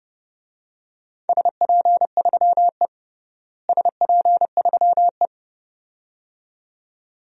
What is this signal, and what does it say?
Morse code, heard as a keyed tone, about 29 wpm, 700 Hz. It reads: SP3E SP3E
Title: DXCC-100 Call Signs - 4 Characters - Repeated 2x - 30wpm